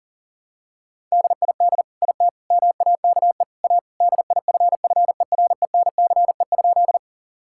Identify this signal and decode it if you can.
Morse code, heard as a keyed tone, about 37 wpm, 700 Hz. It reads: DID IT MAKE A DIFFERENCE?
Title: Sentences from Top 1000 Words - Review Part 5 - 40wpm